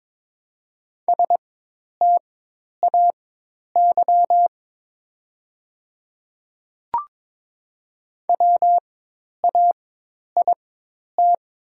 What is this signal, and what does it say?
Morse code, heard as a keyed tone, about 22 wpm, spaced as for 10 wpm, 700 Hz. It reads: STAY WAIT
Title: 4x Character Spacing - Top 500 Words - 22wpm